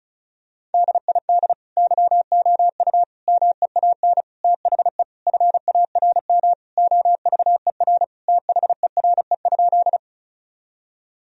Morse code, heard as a keyed tone, about 34 wpm, 700 Hz. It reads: DID YOU MEAN THE FARM OVER THERE?
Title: Sentences from Top 1000 Words - Review Part 2 - 35wpm